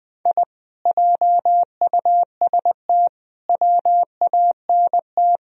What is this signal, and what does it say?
Morse code, heard as a keyed tone, about 22 wpm, 700 Hz. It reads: I JUST WANT
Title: Speed-Racing - Sets of 4 Words 20wpm